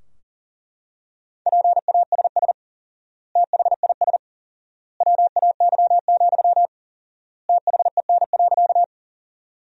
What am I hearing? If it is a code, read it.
Morse code, heard as a keyed tone, about 40 wpm, 700 Hz. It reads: PASS THIS WAY, THEN.
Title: 4x Word-Spacing - Sentences from Top 400 Words - 40wpm